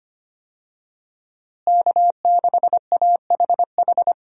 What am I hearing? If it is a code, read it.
Morse code, heard as a keyed tone, about 25 wpm, 700 Hz. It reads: K6AHH
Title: USA Call Signs - 5 Characters - Repeated 3x - 25wpm